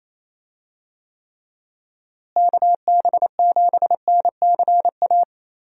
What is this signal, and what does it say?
Morse code, heard as a keyed tone, about 27 wpm, 700 Hz. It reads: KB7NCA
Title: USA Call Signs - 6 Characters - Repeated 2x - 28wpm